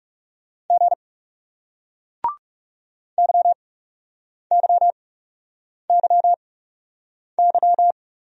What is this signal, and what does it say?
Morse code, heard as a keyed tone, about 39 wpm, spaced as for 8 wpm, 700 Hz. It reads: G YYYY